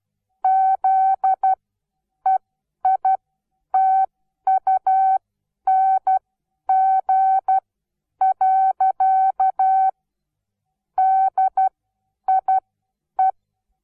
Morse code, heard as a keyed tone, about 12 wpm, spaced as for 9 wpm, 760 Hz. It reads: ZEITUNG. DIE
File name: Onlinelehrgang / morsen / DJ4UF Morsen17